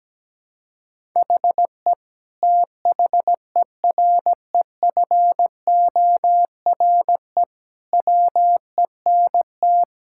Morse code, heard as a keyed tone, about 17 wpm, 700 Hz. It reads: HE THEREFORE WENT
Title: Sentences from Top 1000 Words - Review Part 5 - 17wpm